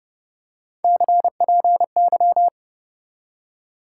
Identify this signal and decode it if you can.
Morse code, heard as a keyed tone, about 32 wpm, 700 Hz. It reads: CPY
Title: Top 90 QSO Elements 30wpm